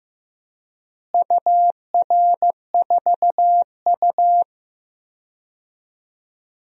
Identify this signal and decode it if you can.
Morse code, heard as a keyed tone, about 15 wpm, 700 Hz. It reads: UR4U
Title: DXCC-100 Call Signs - 4 Characters - Repeated 2x - 15wpm